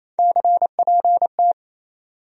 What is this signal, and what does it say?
Morse code, heard as a keyed tone, about 28 wpm, 700 Hz. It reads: CPT
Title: USA Call Sign Postfixes - 3 Letters 28wpm